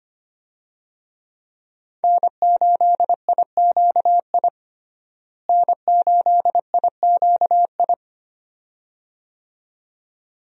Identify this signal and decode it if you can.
Morse code, heard as a keyed tone, about 25 wpm, 700 Hz. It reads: N8IQI N8IQI
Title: USA Call Signs - 5 Characters - Repeated 2x - 25wpm